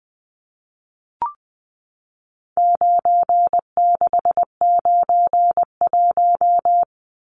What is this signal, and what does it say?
Morse code, heard as a keyed tone, about 20 wpm, 700 Hz. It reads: 9691
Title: Numbers - 4-Digit 20wpm